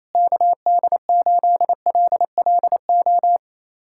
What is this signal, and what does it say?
Morse code, heard as a keyed tone, about 28 wpm, 700 Hz. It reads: KD8LLO
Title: USA Callsign DE Callsign - 6 Characters - 28wpm